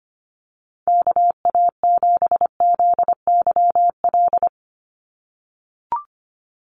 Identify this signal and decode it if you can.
Morse code, heard as a keyed tone, about 25 wpm, 700 Hz. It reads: KA7ZYL
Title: USA Call Signs 25wpm